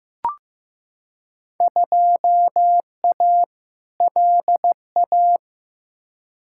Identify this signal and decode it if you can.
Morse code, heard as a keyed tone, about 15 wpm, 700 Hz. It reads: 2A LA
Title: Field Day Exchanges - 15wpm